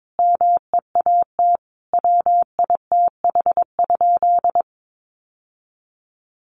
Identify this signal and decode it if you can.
Morse code, heard as a keyed tone, about 22 wpm, 700 Hz. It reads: MEAT WITH?
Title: Sentences from Top 1400 Words 22wpm